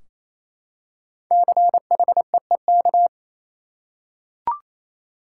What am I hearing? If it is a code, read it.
Morse code, heard as a keyed tone, about 28 wpm, 700 Hz. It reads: CHEEK
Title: Five Letter Words - 28wpm